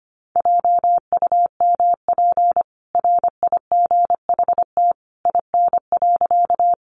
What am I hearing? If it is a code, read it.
Morse code, heard as a keyed tone, about 25 wpm, 700 Hz. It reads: JUMP RIGHT IN.
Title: Sentences from Top 1300 Words 25wpm